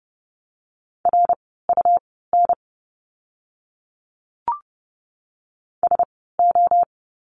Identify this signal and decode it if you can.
Morse code, heard as a keyed tone, about 30 wpm, spaced as for 17 wpm, 700 Hz. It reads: RUN SO
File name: course / 30:10wpm / 115 Lesson 19f - Letter w - Words 30wpm